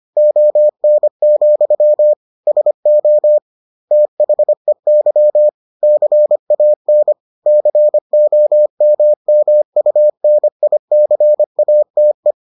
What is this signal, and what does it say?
Morse code, heard as a keyed tone, about 25 wpm, 600 Hz. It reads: ON, SO THEY CAN COMMUNICATE